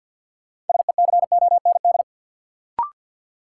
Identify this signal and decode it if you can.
Morse code, heard as a keyed tone, about 53 wpm, 700 Hz. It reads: SECOND
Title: Sentences from Top 300 Words - New Word Drill - 50wpm